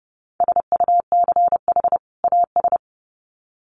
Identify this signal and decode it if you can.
Morse code, heard as a keyed tone, about 30 wpm, 700 Hz. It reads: SUCH AS
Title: Sets of 2 Words 30wpm